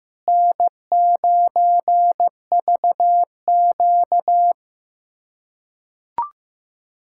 Morse code, heard as a keyed tone, about 15 wpm, 700 Hz. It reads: N9VQ
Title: USA Callsign DE Callsign - 4 Characters - 15wpm